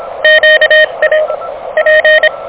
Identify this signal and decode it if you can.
Morse code, heard as a keyed tone, about 24 wpm, 590 Hz, with radio noise. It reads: QA P
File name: OK QRP 2023 / OK QRP 2023